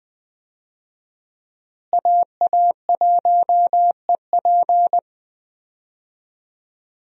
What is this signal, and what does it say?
Morse code, heard as a keyed tone, about 20 wpm, 700 Hz. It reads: AA1EP
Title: USA Call Signs - 5 Characters - Repeated 2x - 20wpm